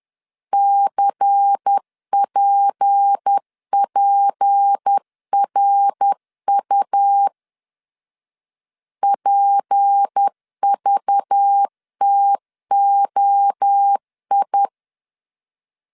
Morse code, heard as a keyed tone, about 10 wpm, 800 Hz. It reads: CPPRU PVTOI
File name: LEZIONI / 6 PKX0 / 6 P 10wpm 2432